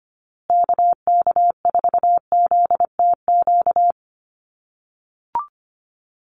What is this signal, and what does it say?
Morse code, heard as a keyed tone, about 25 wpm, 700 Hz. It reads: KK4ZTQ
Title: USA Call Signs Repeated 3x 25wpm